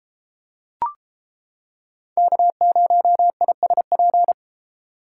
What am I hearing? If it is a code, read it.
Morse code, heard as a keyed tone, about 30 wpm, 700 Hz. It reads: K0ISP